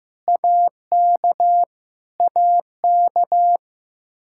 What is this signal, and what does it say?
Morse code, heard as a keyed tone, about 16 wpm, 700 Hz. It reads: AK AK
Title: Practice QSO - Exchange 1 - Beginning - 15wpm